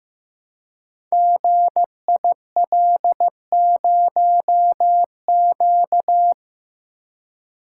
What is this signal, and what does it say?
Morse code, heard as a keyed tone, about 15 wpm, 700 Hz. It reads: GIL0Q
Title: Code Groups - 5 Letter:Numbers - 15wpm